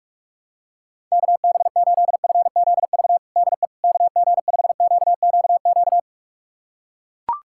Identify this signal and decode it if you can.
Morse code, heard as a keyed tone, about 41 wpm, 700 Hz. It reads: KB8FZV DE KG5QQQ
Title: USA Callsign DE Callsign - 6 Characters - 45wpm